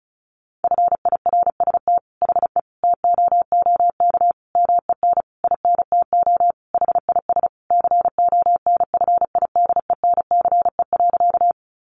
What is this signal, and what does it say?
Morse code, heard as a keyed tone, about 34 wpm, 700 Hz. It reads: FIRST HE TOOK MEN INTO HIS CONFIDENCE.
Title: Sentences from Top 1500 Words 35wpm